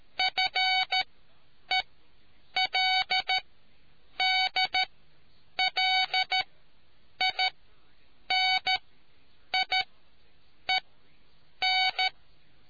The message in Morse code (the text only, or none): FELDLINIEN